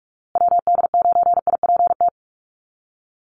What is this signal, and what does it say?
Morse code, heard as a keyed tone, about 44 wpm, 700 Hz. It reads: WD9IPT
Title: USA Call Signs Repeated 3x 45wpm